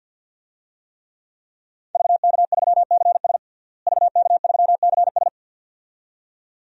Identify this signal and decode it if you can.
Morse code, heard as a keyed tone, about 51 wpm, 700 Hz. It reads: VK3CS VK3CS
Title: DXCC-100 Call Signs - 5 Characters - Repeated 2x - 50wpm